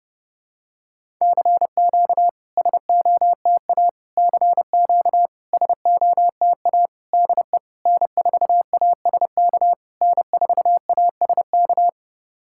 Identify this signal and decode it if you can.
Morse code, heard as a keyed tone, about 30 wpm, 700 Hz. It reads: CQ SOTA CQ SOTA DE N4ASK N4ASK
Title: SOTA QSO Practice 30wpm